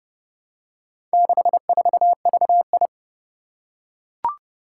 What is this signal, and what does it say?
Morse code, heard as a keyed tone, about 30 wpm, 700 Hz. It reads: 64VI
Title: Four Challenging Letter-Number Combinations 30wpm